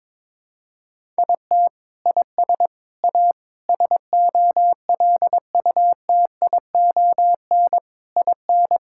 Morse code, heard as a keyed tone, about 22 wpm, 700 Hz. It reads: IT IS A SOLUTION IN
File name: Sentences with 7-Word Phrases - 22wpm